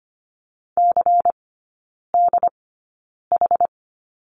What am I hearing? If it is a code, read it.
Morse code, heard as a keyed tone, about 25 wpm, spaced as for 9 wpm, 700 Hz. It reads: CDH
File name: Lesson 17e - Letter u - Three Characters 25wpm